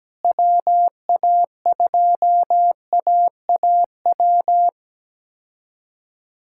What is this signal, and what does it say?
Morse code, heard as a keyed tone, about 17 wpm, 700 Hz. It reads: WA2AAW
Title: K1USN SST QSO Practice - 17wpm